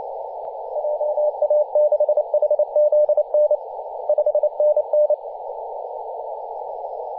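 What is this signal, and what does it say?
Morse code, heard as a keyed tone, about 28 wpm, 600 Hz, with radio noise. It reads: JA6HZN 5NN
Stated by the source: DX station in Papua New Guinea on 21.040 MHz in CW.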